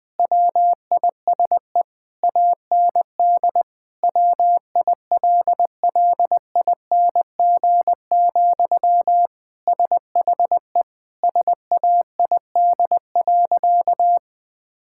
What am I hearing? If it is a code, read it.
Morse code, heard as a keyed tone, about 21 wpm, 700 Hz. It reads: WISE AND WILLING, SHE SAID.
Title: Sentences from Top 1600 Words 20wpm